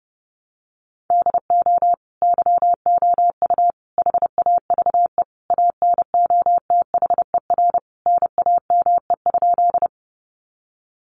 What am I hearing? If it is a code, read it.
Morse code, heard as a keyed tone, about 30 wpm, 700 Hz. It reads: DO YOU HAVE ANOTHER NAME?
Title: Sentences from Top 600 Words 30wpm